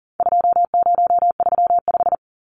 Morse code, heard as a keyed tone, about 40 wpm, 700 Hz. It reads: 2035
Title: Numbers - 4-Digit 40wpm